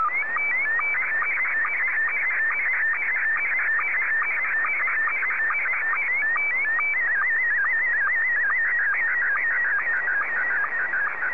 SSTV (slow-scan television), Scottie 1.
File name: sstv / S1-ZL2TJM